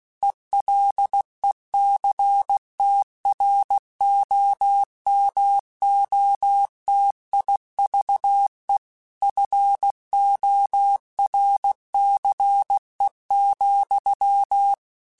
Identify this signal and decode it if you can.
Morse code, heard as a keyed tone, about 16 wpm, 800 Hz. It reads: ELECTROMOTIVE FORCE,
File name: Technical Subjects / Technical Subjects-mp3s / XL @15